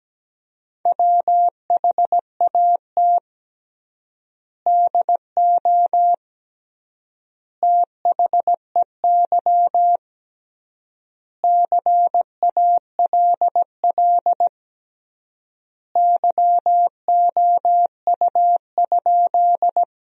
Morse code, heard as a keyed tone, about 17 wpm, 700 Hz. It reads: WHAT DO THEY CALL YOU?
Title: 3x Word-Spacing - Sentences from Top 100 Words - 17wpm